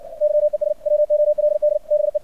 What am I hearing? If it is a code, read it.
Morse code, heard as a keyed tone, about 68 wpm, 600 Hz. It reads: YES WOLD BE
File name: archived / CRSnet